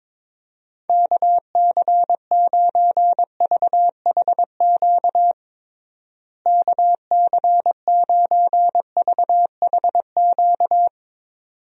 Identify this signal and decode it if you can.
Morse code, heard as a keyed tone, about 22 wpm, 700 Hz. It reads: KC9VHQ KC9VHQ
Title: USA Call Signs - 6 Characters - Repeated 2x - 22wpm